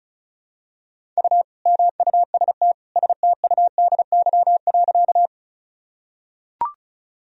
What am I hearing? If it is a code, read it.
Morse code, heard as a keyed tone, about 36 wpm, 700 Hz. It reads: U MUST STUDY.
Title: Sentences from Top 300 Words and Abbreviations 35wpm